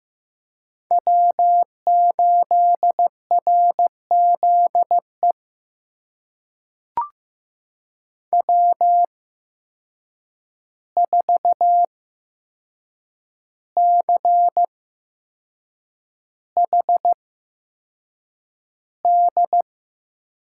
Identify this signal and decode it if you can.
Morse code, heard as a keyed tone, about 15 wpm, spaced as for 6 wpm, 700 Hz. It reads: W8RZE W4CHD
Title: USA Call Signs - 5 Characters - 8x Character Spacing - 15wpm